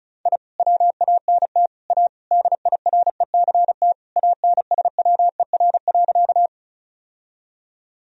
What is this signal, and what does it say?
Morse code, heard as a keyed tone, about 36 wpm, 700 Hz. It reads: I WANT A DIRECT ANSWER.
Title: Sentences from Top 1000 Words - Review Part 2 - 35wpm